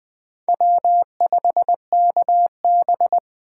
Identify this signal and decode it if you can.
Morse code, heard as a keyed tone, about 20 wpm, 700 Hz. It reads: W5KB
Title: USA Call Signs - 4 Characters - 4x Character Spacing - 20wpm